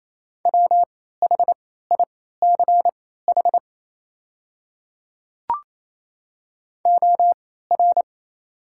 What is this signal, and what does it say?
Morse code, heard as a keyed tone, about 27 wpm, spaced as for 16 wpm, 690 Hz. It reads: WHICH OR
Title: 3x Character Spacing - Top 200 Words - 28wpm